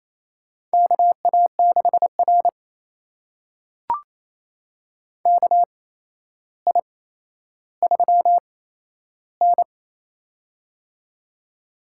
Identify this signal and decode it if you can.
Morse code, heard as a keyed tone, about 27 wpm, spaced as for 12 wpm, 700 Hz. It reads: KA6R KI3N
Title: USA Call Signs - 4 Characters - 8x Character Spacing - 28wpm